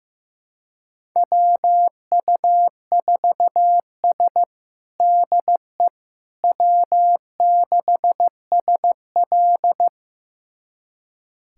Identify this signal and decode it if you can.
Morse code, heard as a keyed tone, about 15 wpm, 700 Hz. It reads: WU4S DE W6SL
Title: USA Callsign DE Callsign - 4 Characters - 15wpm